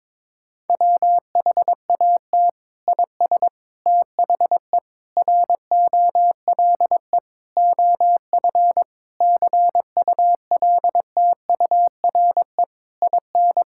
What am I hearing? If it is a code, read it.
Morse code, heard as a keyed tone, about 22 wpm, 700 Hz. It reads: WHAT IS THE ROLE OF CULTURE IN
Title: Sentences from Top 1000 Words - Review Part 6 - 22wpm